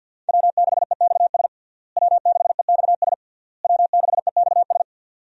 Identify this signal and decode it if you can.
Morse code, heard as a keyed tone, about 52 wpm, 700 Hz. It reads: W6EXS W6EXS W6EXS
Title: USA Call Signs - 5 Characters - Repeated 3x - 50wpm